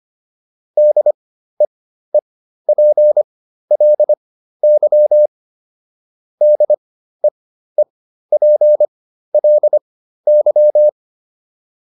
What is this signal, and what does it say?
Morse code, heard as a keyed tone, about 25 wpm, spaced as for 13 wpm, 600 Hz. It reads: DEEPLY DEEPLY